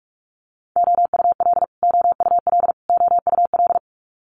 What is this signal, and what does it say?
Morse code, heard as a keyed tone, about 46 wpm, 700 Hz. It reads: OUL OUL OUL
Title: Multi-letter Phonemes Repeated 3x 45wpm